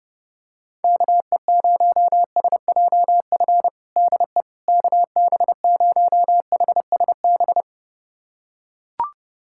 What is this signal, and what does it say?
Morse code, heard as a keyed tone, about 32 wpm, 700 Hz. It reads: KE0SJF DE KB0HSB
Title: USA Callsign DE Callsign - 6 Characters - 30wpm